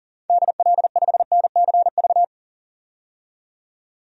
Morse code, heard as a keyed tone, about 40 wpm, 700 Hz. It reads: DL5NCV
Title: DXCC-100 Call Signs - 6 Characters - 40wpm